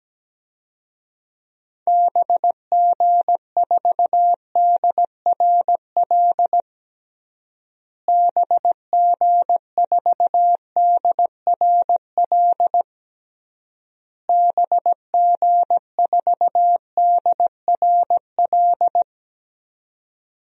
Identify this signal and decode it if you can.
Morse code, heard as a keyed tone, about 17 wpm, 700 Hz. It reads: BG4DRL BG4DRL BG4DRL
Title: DXCC-100 Call Signs - 6 Characters - Repeated 3x - 17wpm